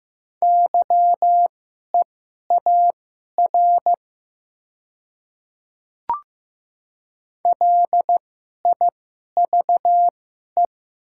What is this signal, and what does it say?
Morse code, heard as a keyed tone, about 15 wpm, spaced as for 11 wpm, 700 Hz. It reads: YEAR LIVE
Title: 2x Character Spacing - Top 200 Words - 15wpm